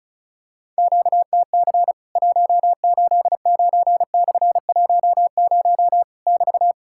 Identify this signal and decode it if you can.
Morse code, heard as a keyed tone, about 36 wpm, 700 Hz. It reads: QTC 189/10 <BT>
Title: Worked All Europe DX Contest - QTC Exchange - 35wpm